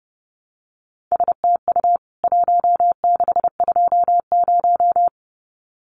Morse code, heard as a keyed tone, about 30 wpm, 700 Hz. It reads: STU 1620
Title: CWT Exchange 30wpm